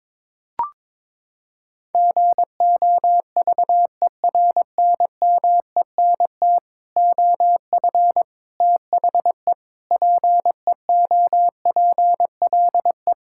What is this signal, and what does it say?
Morse code, heard as a keyed tone, about 22 wpm, 700 Hz. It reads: GOVERNMENT OF THE PEOPLE
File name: Sets of 9 Words - 22wpm